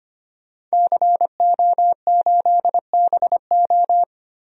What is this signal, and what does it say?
Morse code, heard as a keyed tone, about 25 wpm, 700 Hz. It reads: CO8BO
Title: DXCC-100 Call Signs - 5 Characters - 25wpm